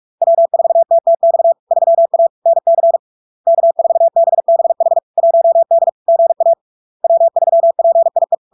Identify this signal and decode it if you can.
Morse code, heard as a keyed tone, about 46 wpm, 670 Hz. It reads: W4TTX 3A NC K4BBH 1D GA W3PIE
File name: LARK-FD-2022-Log-in-CW-45wpm-670-Hz